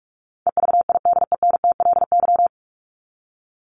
Morse code, heard as a keyed tone, about 44 wpm, 700 Hz. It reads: EVIDENTLY
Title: LY Suffix - Top 100 Words with Base Words 45wpm